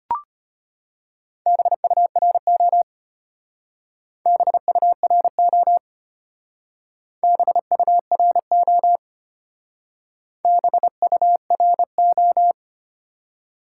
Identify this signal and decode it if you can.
Morse code, heard as a keyed tone, about 31 wpm, 700 Hz. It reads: BURO BURO BURO BURO